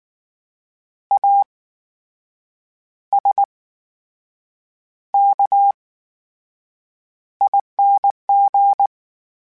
Morse code, heard as a keyed tone, about 19 wpm, spaced as for 7 wpm, 800 Hz. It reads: ASKING
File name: Special INGs / Special INGs-mp3s / VerySlowToFaster Ing @18